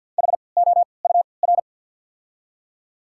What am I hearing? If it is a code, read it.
Morse code, heard as a keyed tone, about 48 wpm, spaced as for 28 wpm, 690 Hz. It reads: HOUR
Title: 3x Character Spacing - Top 500 Words - 50wpm